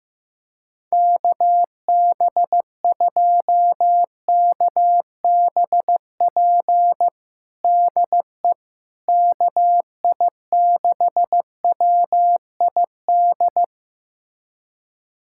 Morse code, heard as a keyed tone, about 16 wpm, 700 Hz. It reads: KB2KBP DE KI6WID
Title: USA Callsign DE Callsign - 6 Characters - 15wpm